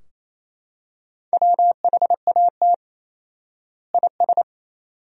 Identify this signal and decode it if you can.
Morse code, heard as a keyed tone, about 28 wpm, 700 Hz. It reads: WHAT IS
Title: 4x Word-Spacing - Sentences from Top 100 Words - 28wpm